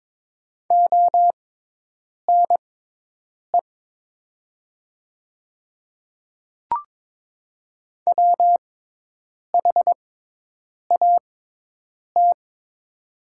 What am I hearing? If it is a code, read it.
Morse code, heard as a keyed tone, about 22 wpm, spaced as for 8 wpm, 700 Hz. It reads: ONE WHAT
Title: 6x Character Spacing - Top 100 Words - 22wpm